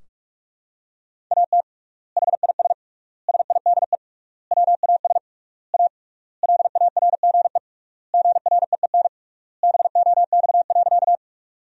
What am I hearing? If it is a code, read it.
Morse code, heard as a keyed tone, about 46 wpm, 700 Hz. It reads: AT HIS SIDE WAS A LARGE GREEN BOX.
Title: 3x Word-Spacing - Sentences from Top 500 Words - 45wpm